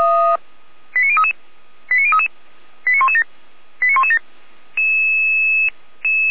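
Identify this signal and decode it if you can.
Five-tone selective call (ZVEI), heard as quick runs of five tones, tones of about 70 ms. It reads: 89033 89033 89117 89117
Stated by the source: ZVEI five-tone selective call received on an SDR.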